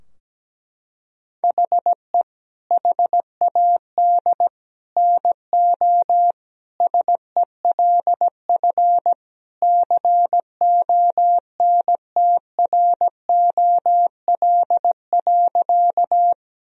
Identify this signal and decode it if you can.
Morse code, heard as a keyed tone, about 17 wpm, 700 Hz. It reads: HE HAD NO SELF CONTROL.
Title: Sentences from Top 1000 Words - Review Part 3 - 17wpm